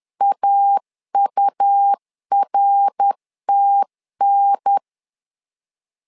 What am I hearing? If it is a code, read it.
Morse code, heard as a keyed tone, about 10 wpm, 800 Hz. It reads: AURTN
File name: LEZIONI / 3 MWRO / 3 MWR 10wpm 1952